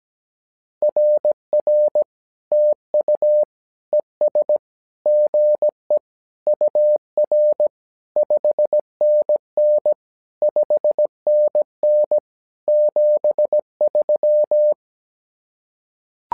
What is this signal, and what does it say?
Morse code, heard as a keyed tone, about 17 wpm, 600 Hz. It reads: RR TU ES GE UR 5NN 5NN 73
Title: SOTA QSO Practice 17wpm